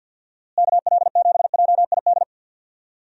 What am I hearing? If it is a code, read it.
Morse code, heard as a keyed tone, about 46 wpm, 700 Hz. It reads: KL7JID
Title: DXCC-100 Call Signs - 6 Characters - Repeated 2x - 50wpm